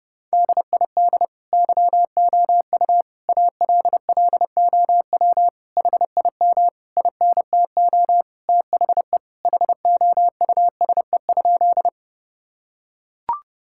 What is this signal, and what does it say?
Morse code, heard as a keyed tone, about 30 wpm, 700 Hz. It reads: DID YOU ALLOW HIM INTO THE HOUSE?
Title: Sentences from Top 1000 Words - Review Part 5 - 30wpm